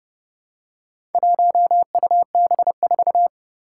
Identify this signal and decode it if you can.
Morse code, heard as a keyed tone, about 30 wpm, 700 Hz. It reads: 1UB4
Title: Four Challenging Letter-Number Combinations 30wpm